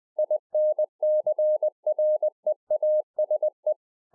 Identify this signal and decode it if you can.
Morse code, heard as a keyed tone, about 20 wpm, 610 Hz. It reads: INCREASE
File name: 20WPM